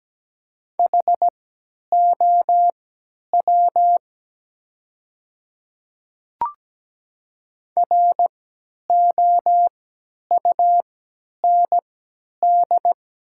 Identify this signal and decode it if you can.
Morse code, heard as a keyed tone, about 17 wpm, spaced as for 10 wpm, 700 Hz. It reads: HOW ROUND
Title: 3x Character Spacing - Top 200 Words - 17wpm